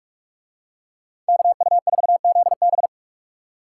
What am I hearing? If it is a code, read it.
Morse code, heard as a keyed tone, about 46 wpm, 700 Hz. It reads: KU4ZB